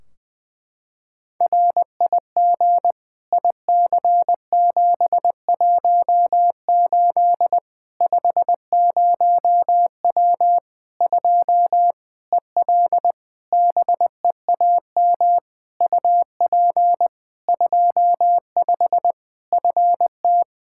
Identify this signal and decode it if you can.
Morse code, heard as a keyed tone, about 20 wpm, 700 Hz. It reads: RIG IC718 50W 2 EL BEAM UP 25 FT
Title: K1USN SST QSO Practice 20wpm